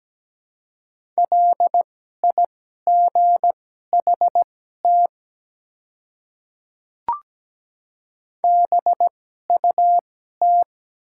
Morse code, heard as a keyed tone, about 17 wpm, spaced as for 12 wpm, 700 Hz. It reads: LIGHT BUT
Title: 2x Character Spacing - Top 300 Words - 17wpm